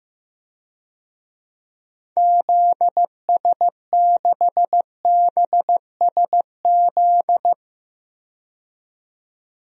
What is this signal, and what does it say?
Morse code, heard as a keyed tone, about 15 wpm, 700 Hz. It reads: ZS6BSZ
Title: DXCC-100 Call Signs - 6 Characters - 15wpm